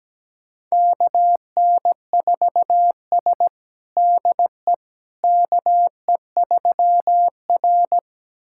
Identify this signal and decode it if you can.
Morse code, heard as a keyed tone, about 17 wpm, 700 Hz. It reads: KN4S DE KE3R
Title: USA Callsign DE Callsign - 4 Characters - 17wpm